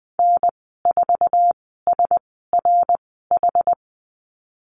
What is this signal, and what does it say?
Morse code, heard as a keyed tone, about 20 wpm, spaced as for 14 wpm, 700 Hz. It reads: N4SRH
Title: 069 Lesson 12g - Letter d - Call Signs 20wpm